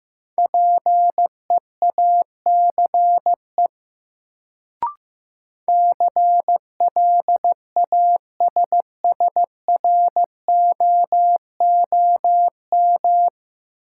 Morse code, heard as a keyed tone, about 15 wpm, 700 Hz. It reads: PEACE CLASSROOM
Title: Sentences from Top 1500 Words - New Word Drill Alt - 15wpm